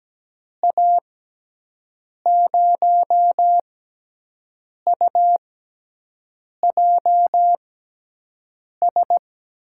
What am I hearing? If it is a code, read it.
Morse code, heard as a keyed tone, about 17 wpm, 700 Hz. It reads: A 0 U J S
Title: USA Call Signs - 6 Characters - 6x Character Spacing - 17wpm